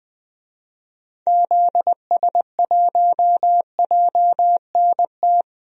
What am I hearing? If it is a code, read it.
Morse code, heard as a keyed tone, about 20 wpm, 700 Hz. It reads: ZS1JNT
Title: DXCC-100 Call Signs - 6 Characters - Repeated 3x - 20wpm